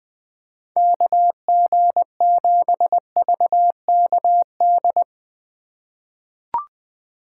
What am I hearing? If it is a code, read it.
Morse code, heard as a keyed tone, about 21 wpm, 700 Hz. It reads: KG7VKD
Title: USA Call Signs - 6 Characters - 3x Character Spacing - 20wpm